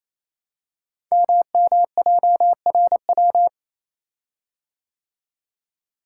Morse code, heard as a keyed tone, about 27 wpm, 700 Hz. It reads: MMJRW
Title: Code Groups - 5 Letters - 28wpm